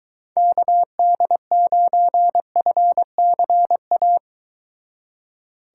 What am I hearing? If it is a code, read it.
Morse code, heard as a keyed tone, about 22 wpm, 700 Hz. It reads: KD9FCA